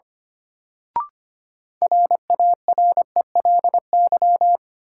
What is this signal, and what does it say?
Morse code, heard as a keyed tone, about 25 wpm, 700 Hz. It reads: RARELY
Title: Sentences from Top 1400 Words - New Word Drill - 25wpm